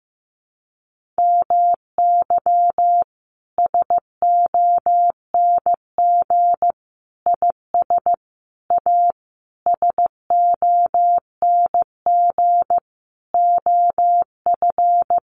Morse code, heard as a keyed tone, about 15 wpm, 700 Hz. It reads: MY SONG IS A SONG OF
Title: Sentences from Top 1500 Words 15wpm